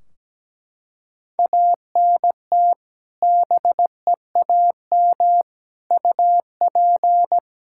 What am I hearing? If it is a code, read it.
Morse code, heard as a keyed tone, about 17 wpm, 700 Hz. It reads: ANT BEAM UP
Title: Practice QSO - Exchange 2 - Info - 17wpm